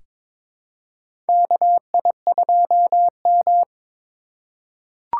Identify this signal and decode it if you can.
Morse code, heard as a keyed tone, about 22 wpm, 700 Hz. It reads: KI2M
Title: USA Call Signs - 4x Character Spacing - 22wpm